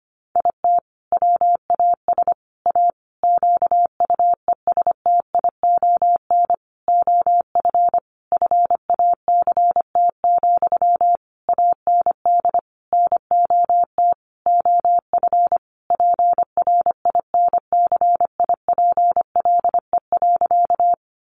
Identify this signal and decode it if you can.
Morse code, heard as a keyed tone, about 25 wpm, 700 Hz. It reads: IT WAS A QUESTION OF FACT, AND NOT OF PRINCIPLE.
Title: Sentences from Top 1400 Words 25wpm